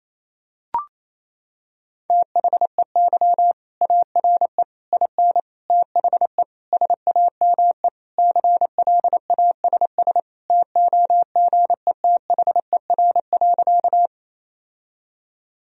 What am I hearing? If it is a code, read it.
Morse code, heard as a keyed tone, about 28 wpm, 700 Hz. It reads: THEY ARE IN THE SAME CLASS TOGETHER.
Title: Sentences from Top 1000 Words - Review Part 2 - 28wpm